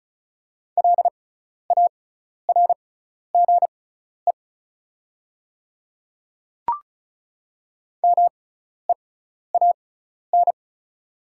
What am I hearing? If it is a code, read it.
Morse code, heard as a keyed tone, about 34 wpm, spaced as for 12 wpm, 690 Hz. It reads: LARGE MEAN
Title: 6x Character Spacing - Top 200 Words - 35wpm